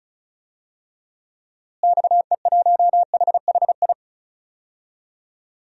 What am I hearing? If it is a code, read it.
Morse code, heard as a keyed tone, about 36 wpm, 700 Hz. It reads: XE1HHI
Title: DXCC-100 Call Signs - 6 Characters - Repeated 3x - 35wpm